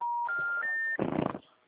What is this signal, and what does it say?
Special information tone, with noise.